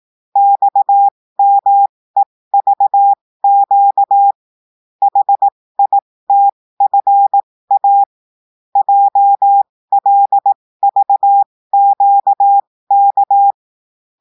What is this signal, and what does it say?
Morse code, heard as a keyed tone, about 18 wpm, spaced as for 15 wpm, 800 Hz. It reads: XMEVQ HITFA JLVQK
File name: cours-du-lundi files / cours-1580-texte-chiffre-15-mpm